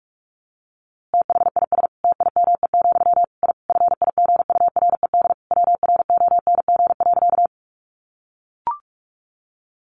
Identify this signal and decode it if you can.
Morse code, heard as a keyed tone, about 46 wpm, 700 Hz. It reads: THIS TIME, I FIGURED WRONG.
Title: A Sound Of Thunder 45wpm